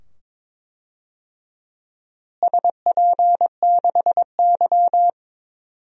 Morse code, heard as a keyed tone, about 22 wpm, 700 Hz. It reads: SP6Y
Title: DXCC-100 Call Signs - 4 Characters - Repeated 2x - 22wpm